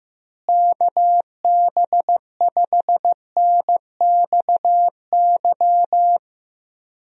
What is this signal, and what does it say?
Morse code, heard as a keyed tone, about 15 wpm, 700 Hz. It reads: KB5NXY
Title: USA Call Signs - 6 Characters - Repeated 3x - 15wpm